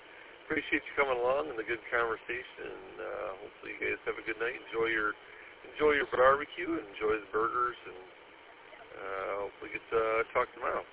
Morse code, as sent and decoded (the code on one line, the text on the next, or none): none